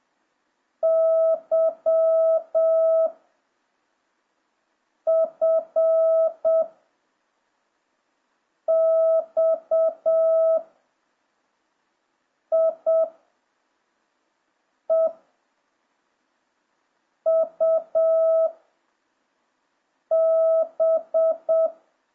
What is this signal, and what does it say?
Morse code, heard as a keyed tone, about 7 wpm, 650 Hz. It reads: Y F X I E U B